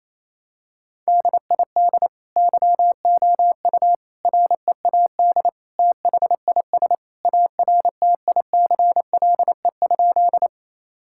Morse code, heard as a keyed tone, about 28 wpm, 700 Hz. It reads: DID YOU READ THIS ARTICLE?
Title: Sentences from Top 1000 Words - Review Part 5 - 28wpm